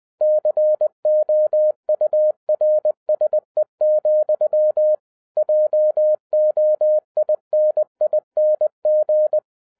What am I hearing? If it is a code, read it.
Morse code, heard as a keyed tone, about 20 wpm, 610 Hz. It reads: COURSE, JOINING
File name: 2012-01-mp3 / n16-THROUGH EIGHT MONTHS-20wpm